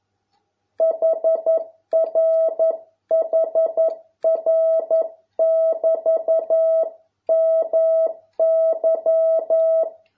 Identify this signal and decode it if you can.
Morse code, heard as a keyed tone, about 11 wpm, 650 Hz. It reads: HR HR <BT> MY